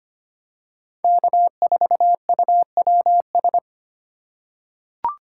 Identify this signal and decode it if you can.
Morse code, heard as a keyed tone, about 25 wpm, 700 Hz. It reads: K4UWS
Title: USA Call Signs - 5 Characters - 2x Character Spacing - 25wpm